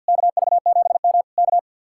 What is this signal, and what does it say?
Morse code, heard as a keyed tone, about 62 wpm, 700 Hz. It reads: KV7M K
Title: Speed-Racing Common QSO Phrases Encore - 50wpm